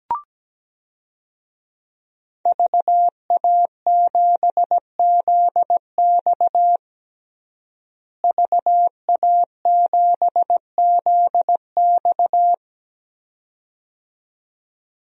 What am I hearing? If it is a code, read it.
Morse code, heard as a keyed tone, about 17 wpm, 700 Hz. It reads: VA7ZX VA7ZX
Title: DXCC-100 Call Signs - 5 Characters - Repeated 2x - 17wpm